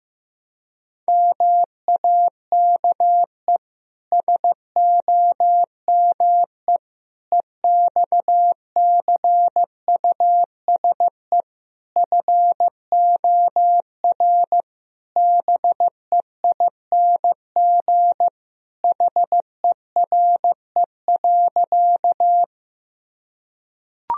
Morse code, heard as a keyed tone, about 15 wpm, 700 Hz. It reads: MAKE SOME EXCUSE FOR BEING HERE.
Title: Sentences from Top 1000 Words - Review Part 3 - 15wpm